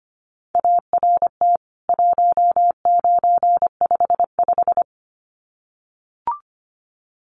Morse code, heard as a keyed tone, about 25 wpm, 700 Hz. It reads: ART 1955
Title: CWT Exchange 25wpm